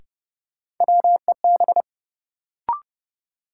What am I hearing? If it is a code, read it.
Morse code, heard as a keyed tone, about 31 wpm, 700 Hz. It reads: WEB
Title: Sentences from Top 1000 Words - Review Part 4 - New Word Drill - 30wpm